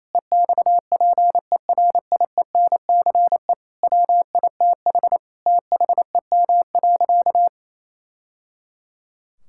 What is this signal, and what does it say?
Morse code, heard as a keyed tone, about 28 wpm, 700 Hz. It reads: EXPERIENCE WITH THEM.
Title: Sentences from Top 1000 Words - Review Part 6 - 28wpm